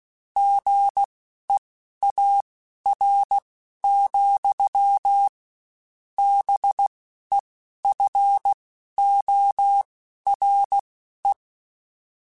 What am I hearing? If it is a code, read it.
Morse code, heard as a keyed tone, about 16 wpm, spaced as for 11 wpm, 800 Hz. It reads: GEAR, BEFORE
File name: Spaced / Chuckles Spaced-mp3s / BrainMouth sp @15